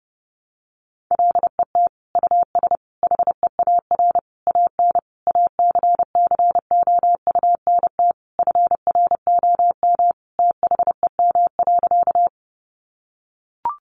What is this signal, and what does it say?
Morse code, heard as a keyed tone, about 30 wpm, 700 Hz. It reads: LET US HEAR AN ACCOUNT FROM THEM.
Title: Sentences from Top 900 Words 30wpm